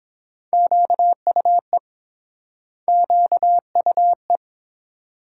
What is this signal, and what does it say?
Morse code, heard as a keyed tone, about 28 wpm, 700 Hz. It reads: QUE QUE